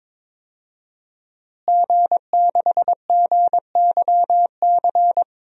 Morse code, heard as a keyed tone, about 21 wpm, 700 Hz. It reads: G6GYC
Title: DXCC-100 Call Signs - 5 Characters - Repeated 2x - 22wpm